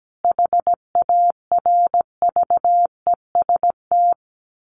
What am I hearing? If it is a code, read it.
Morse code, heard as a keyed tone, about 17 wpm, 700 Hz. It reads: HARVEST
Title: A Sound Of Thunder 17wpm